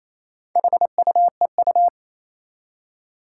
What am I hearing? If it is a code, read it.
Morse code, heard as a keyed tone, about 27 wpm, 700 Hz. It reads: HUEU
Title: Four Challenging Letter-Number Combinations 28wpm